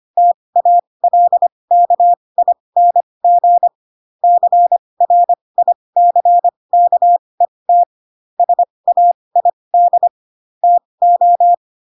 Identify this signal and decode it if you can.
Morse code, heard as a keyed tone, about 25 wpm, spaced as for 20 wpm, 700 Hz. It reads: TALKING CRICKET SAID TO
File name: Pinocchio / Pinocchio ch7 15 25 ik1hge